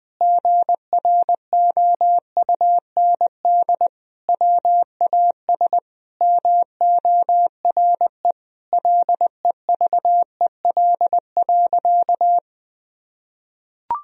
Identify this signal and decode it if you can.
Morse code, heard as a keyed tone, about 20 wpm, 700 Hz. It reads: GROUND WAS MORE LEVEL.
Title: Sentences from Top 1000 Words - Review Part 4 - 20wpm